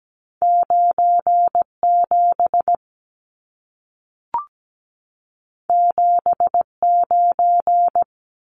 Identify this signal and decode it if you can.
Morse code, heard as a keyed tone, about 17 wpm, 700 Hz. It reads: 97 79
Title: Numbers - 2-Digit 17wpm